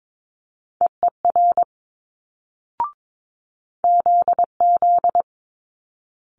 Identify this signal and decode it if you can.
Morse code, heard as a keyed tone, about 22 wpm, 700 Hz. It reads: EER ZZ
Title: Multi-letter Phonemes Quick Fire 22wpm